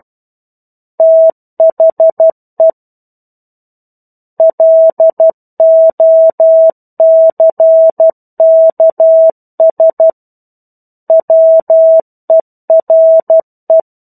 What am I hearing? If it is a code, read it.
Morse code, heard as a keyed tone, about 12 wpm, 650 Hz. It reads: THE LOCKS WERE
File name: 01-Chapter-01 12 WPM